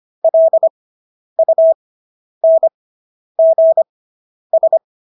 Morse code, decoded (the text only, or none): LUNGS